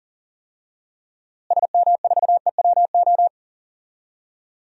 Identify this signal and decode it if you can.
Morse code, heard as a keyed tone, about 40 wpm, 700 Hz. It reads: SM4EWO
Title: DXCC-100 Call Signs - 6 Characters - Repeated 3x - 40wpm